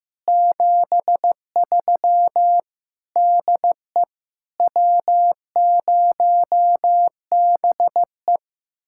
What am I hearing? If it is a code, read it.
Morse code, heard as a keyed tone, about 15 wpm, 700 Hz. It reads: 73 DE W0BE
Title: Speed-Racing Common QSO Phrases Encore - 15wpm